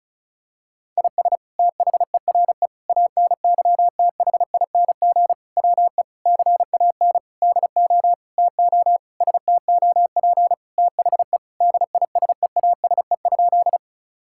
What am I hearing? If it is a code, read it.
Morse code, heard as a keyed tone, about 34 wpm, 700 Hz. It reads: IS THERE ANYTHING WE CAN DO TO STOP THE DISEASE?
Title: Sentences from Top 1000 Words - Review Part 6 - 35wpm